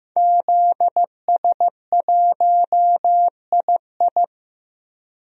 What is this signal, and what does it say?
Morse code, heard as a keyed tone, about 16 wpm, 700 Hz. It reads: ZS1II
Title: DXCC-100 Call Signs - 5 Characters - Repeated 3x - 15wpm